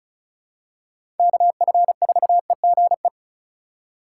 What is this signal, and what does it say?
Morse code, heard as a keyed tone, about 34 wpm, 690 Hz. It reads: KF4EGE
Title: USA Call Signs - 6 Characters - 2x Character Spacing - 35wpm